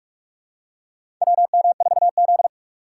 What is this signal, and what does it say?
Morse code, heard as a keyed tone, about 46 wpm, 700 Hz. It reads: WM4Z
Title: USA Call Signs - 4 Characters - 2x Character Spacing - 45wpm